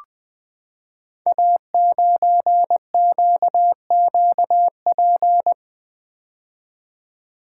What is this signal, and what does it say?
Morse code, heard as a keyed tone, about 20 wpm, 700 Hz. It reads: A9QQP
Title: Code Groups - 5 Letter:Numbers - 20wpm